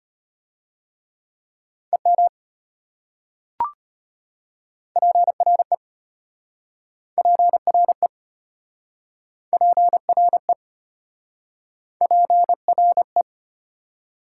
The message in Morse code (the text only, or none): EM PRE PRE PRE PRE